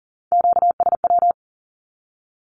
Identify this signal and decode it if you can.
Morse code, heard as a keyed tone, about 39 wpm, 700 Hz. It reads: QSW
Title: Q-Codes with Spoken Meaning - 40wpm